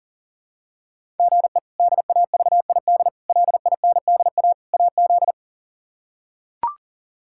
Morse code, heard as a keyed tone, about 40 wpm, 700 Hz. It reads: GE DAVID LINDA AZ
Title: K1USN SST QSO Practice - 40wpm